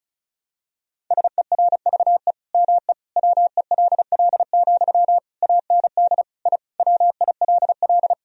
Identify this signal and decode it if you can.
Morse code, heard as a keyed tone, about 34 wpm, 700 Hz. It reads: SERVE ME WELL, AND I WILL
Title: Sentences from Top 1000 Words - Review Part 2 - 35wpm